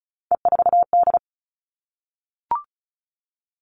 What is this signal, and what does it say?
Morse code, heard as a keyed tone, about 34 wpm, 700 Hz. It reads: E4D
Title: All Cut Numbers - 3-Digit 35wpm